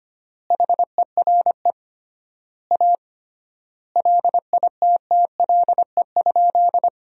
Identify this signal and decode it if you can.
Morse code, heard as a keyed tone, about 25 wpm, 700 Hz. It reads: HERE A LITTLE?
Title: 3x Word-Spacing - Sentences from Top 500 Words - 25wpm